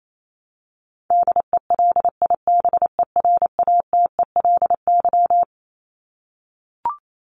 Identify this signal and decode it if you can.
Morse code, heard as a keyed tone, about 28 wpm, 700 Hz. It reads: DELIBERATELY
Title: Top 1751-2000 Words 28wpm